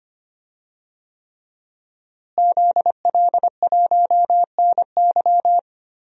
Morse code, heard as a keyed tone, about 25 wpm, 700 Hz. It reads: ZL1NY
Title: DXCC-100 Call Signs - 5 Characters - Repeated 3x - 25wpm